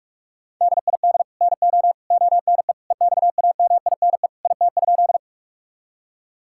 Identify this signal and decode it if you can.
Morse code, heard as a keyed tone, about 42 wpm, 700 Hz. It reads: DID NO ONE EXAMINE IT?
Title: Sentences from Top 1600 Words 45wpm